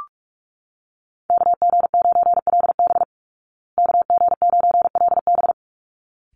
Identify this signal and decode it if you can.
Morse code, heard as a keyed tone, about 44 wpm, 700 Hz. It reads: KG9LB KG9LB
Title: Call Signs Repeated 3x - 45wpm